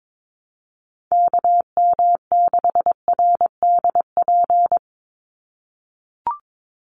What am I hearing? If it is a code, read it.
Morse code, heard as a keyed tone, about 22 wpm, 700 Hz. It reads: KM6RDP
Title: USA Call Signs 22wpm